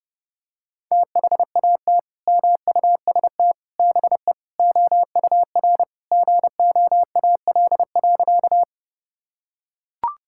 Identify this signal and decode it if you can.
Morse code, heard as a keyed tone, about 28 wpm, 700 Hz. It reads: THAT MUST BE OUR GOAL.
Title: Sentences from Top 1000 Words - Review Part 5 - 30wpm